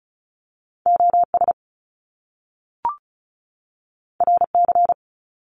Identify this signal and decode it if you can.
Morse code, heard as a keyed tone, about 36 wpm, 700 Hz. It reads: O S R C
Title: Top Two Letter Combinations in Words 35wpm